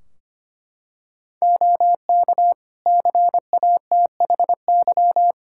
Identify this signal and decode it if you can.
Morse code, heard as a keyed tone, about 25 wpm, 700 Hz. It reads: OK CATHY
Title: Practice QSO - Exchange 2 - Info - 25wpm